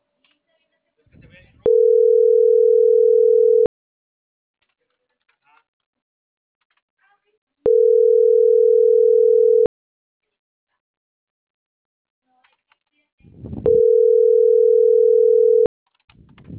Ringback tone (North America), 440 + 480 Hz.